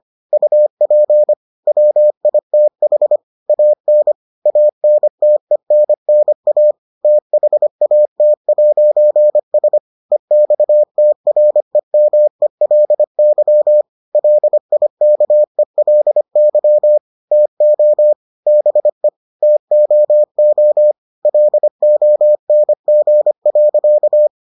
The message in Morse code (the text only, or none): UP WITH AN ANTENNA THAT'S EXTREMELY LIKELY TO BE TOO LONG.